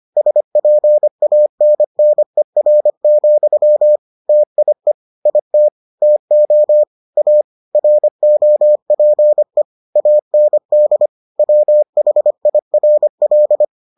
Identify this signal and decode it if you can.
Morse code, heard as a keyed tone, about 25 wpm, 600 Hz. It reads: SPANNER, TIE IT TO A ROPE AND WHIRL